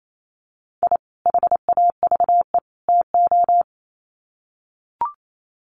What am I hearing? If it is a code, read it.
Morse code, heard as a keyed tone, about 28 wpm, 700 Hz. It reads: I HAVE TO
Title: Sets of 3 Words 28wpm